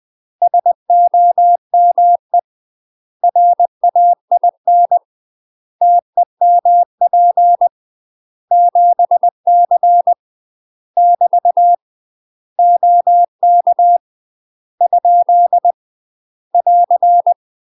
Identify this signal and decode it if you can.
Morse code, heard as a keyed tone, about 20 wpm, 700 Hz. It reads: SOME RAIN TEMP 7C <BT> OK ? <AR>
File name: UK MASTER QSOs TRAINING/20WPM